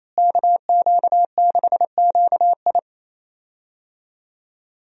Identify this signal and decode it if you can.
Morse code, heard as a keyed tone, about 28 wpm, 700 Hz. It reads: KQ6QI
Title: USA Call Signs - 5 Characters - Repeated 2x - 28wpm